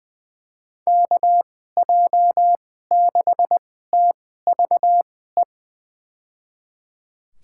Morse code, heard as a keyed tone, about 20 wpm, spaced as for 14 wpm, 700 Hz. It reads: KJ6TVE
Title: USA Call Signs - 2x Character Spacing - 20wpm